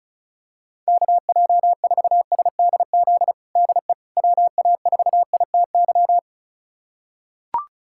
Morse code, heard as a keyed tone, about 36 wpm, 700 Hz. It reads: KJ4SDZ DE WA4ITY
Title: USA Callsign DE Callsign - 6 Characters - 35wpm